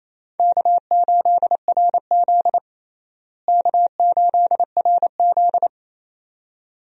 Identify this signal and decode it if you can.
Morse code, heard as a keyed tone, about 27 wpm, 700 Hz. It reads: K8RZ K8RZ
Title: USA Call Signs - 4 Characters - Repeated 2x - 28wpm